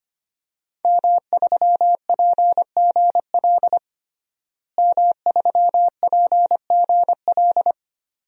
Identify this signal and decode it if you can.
Morse code, heard as a keyed tone, about 25 wpm, 700 Hz. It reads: M3PGL M3PGL
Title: DXCC-100 Call Signs - 5 Characters - Repeated 2x - 25wpm